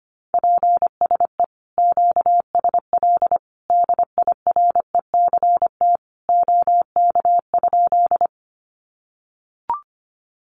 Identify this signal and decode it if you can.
Morse code, heard as a keyed tone, about 25 wpm, 700 Hz. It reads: PSE QSL DIRECT OK?
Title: Common QSO Phrases 25wpm